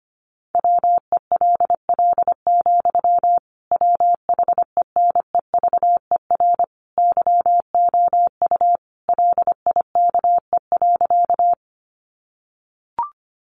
Morse code, heard as a keyed tone, about 25 wpm, 700 Hz. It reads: WELL, WHENEVER YOU LIKE.
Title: Sentences from Top 1200 Words 25wpm